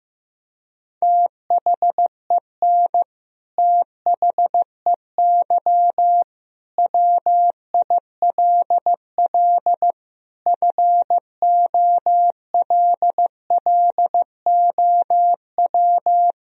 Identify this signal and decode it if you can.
Morse code, heard as a keyed tone, about 15 wpm, 700 Hz. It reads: THEN THEY WILL FOLLOW
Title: Sentences from Top 200 Words and Abbreviations 15wpm